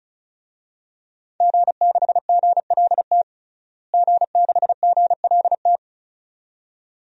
Morse code, heard as a keyed tone, about 36 wpm, 700 Hz. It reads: G6GLT G6GLT
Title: DXCC-100 Call Signs - 5 Characters - Repeated 2x - 35wpm